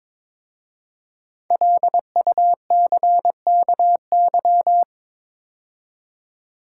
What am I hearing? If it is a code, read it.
Morse code, heard as a keyed tone, about 22 wpm, 700 Hz. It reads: LUCKY
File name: 5 Letter Words - First 3 Letters Given - 22wpm